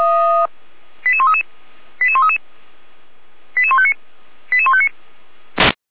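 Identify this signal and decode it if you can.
Five-tone selective call (ZVEI), heard as quick runs of five tones, tones of about 80 ms. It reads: 88133 88133 88159 88159
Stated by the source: ZVEI five-tone selective call received by SDR, fire brigade Feuerwehr Orlamünde.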